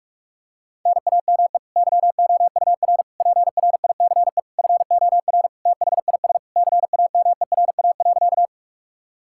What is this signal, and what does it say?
Morse code, heard as a keyed tone, about 46 wpm, 700 Hz. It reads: NAME YOUR PRICE FOR THIS CAMERA.
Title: Sentences from Top 1000 Words - Review Part 5 - 45wpm